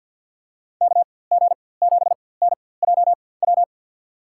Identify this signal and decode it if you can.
Morse code, heard as a keyed tone, about 47 wpm, spaced as for 23 wpm, 700 Hz. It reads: KG7NJW
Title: USA Call Signs - 6 Characters - 4x Character Spacing - 50wpm